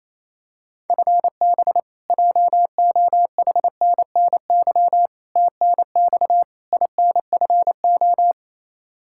Morse code, heard as a keyed tone, about 28 wpm, 700 Hz. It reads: FB JOHNNY TNX INFO
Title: Practice QSO - Exchange 3 - More Info - 28wpm